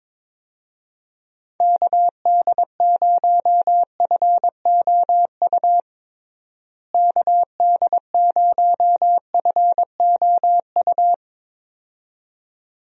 Morse code, heard as a keyed tone, about 22 wpm, 700 Hz. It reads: KD0FOU KD0FOU
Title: USA Call Signs - 6 Characters - Repeated 2x - 22wpm